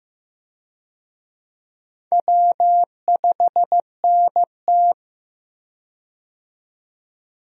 Morse code, heard as a keyed tone, about 15 wpm, 700 Hz. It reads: W5NT
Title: USA Call Signs - 4 Characters - 15wpm